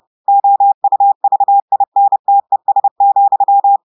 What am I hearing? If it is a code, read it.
Morse code, heard as a keyed tone, about 30 wpm, 800 Hz. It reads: OUVINTES,
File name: audios / portugais / portugais texte en clair 2 vitesse 30 mots minutes